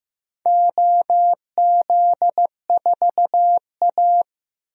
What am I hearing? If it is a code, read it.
Morse code, heard as a keyed tone, about 15 wpm, 700 Hz. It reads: OZ4A
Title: DXCC-100 Call Signs - 4 Characters - 15wpm